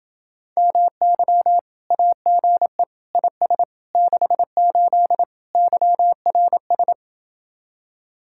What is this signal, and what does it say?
Morse code, heard as a keyed tone, about 30 wpm, 700 Hz. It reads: MY AGE IS 68 YRS